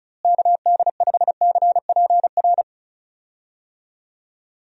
Morse code, heard as a keyed tone, about 36 wpm, 700 Hz. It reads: KD5CPR
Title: USA Call Signs - 6 Characters - Repeated 2x - 35wpm